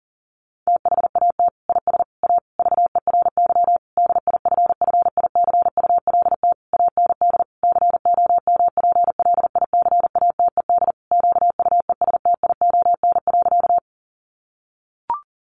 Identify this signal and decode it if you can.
Morse code, heard as a keyed tone, about 40 wpm, 700 Hz. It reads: THAT IS A VERY DIFFICULT AND COMPLICATED QUESTION.
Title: Sentences from Top 1200 Words 40wpm